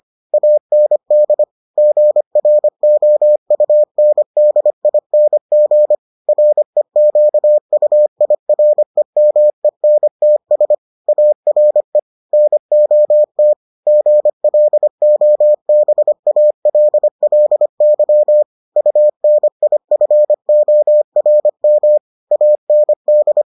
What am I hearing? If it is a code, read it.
Morse code, heard as a keyed tone, about 25 wpm, 600 Hz. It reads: AND GROUNDING REQUIREMENTS ARE NOT GLOBALLY UNIFORM AND